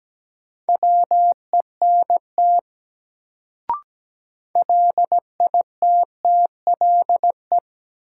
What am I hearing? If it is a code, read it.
Morse code, heard as a keyed tone, about 17 wpm, 700 Hz. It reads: WENT LITTLE
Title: Sets of 2 Words - New Word Drill - 17wpm